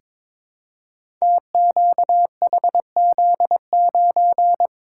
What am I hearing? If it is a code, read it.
Morse code, heard as a keyed tone, about 23 wpm, 690 Hz. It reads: TQHZ9
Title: Code Groups - 5 Letter:Numbers - 22wpm